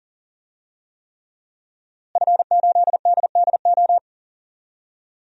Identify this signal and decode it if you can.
Morse code, heard as a keyed tone, about 40 wpm, 700 Hz. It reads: F8DDO
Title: DXCC-100 Call Signs - 5 Characters - Repeated 3x - 40wpm